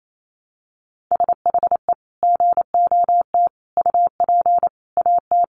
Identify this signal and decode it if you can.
Morse code, heard as a keyed tone, about 28 wpm, 700 Hz. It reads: SHE GOT UP AT
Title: Sentences from Top 1100 Words 28wpm